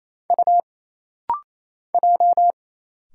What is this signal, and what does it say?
Morse code, heard as a keyed tone, about 28 wpm, 700 Hz. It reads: U J
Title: Single Letters - Rapid-Fire 28wpm